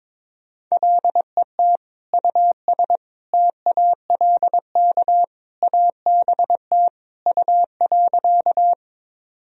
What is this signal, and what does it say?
Morse code, heard as a keyed tone, about 22 wpm, 700 Hz. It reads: LET US TALK ABT U.
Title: Sentences from Top 400 Words and Abbreviations 22wpm